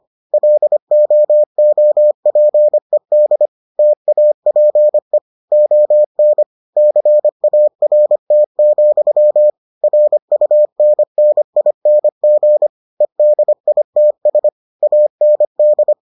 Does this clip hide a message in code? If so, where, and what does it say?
Yes: Morse code from 0.3 to 15.9 s, heard as a keyed tone, about 25 wpm, 600 Hz. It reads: LOOPED TAPE ON CART, RUNNING EDITS AND